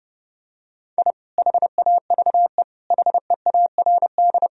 Morse code, heard as a keyed tone, about 32 wpm, 700 Hz. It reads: I HAVE HEARD
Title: Sentences from Top 1000 Words - Review Part 3 - 30wpm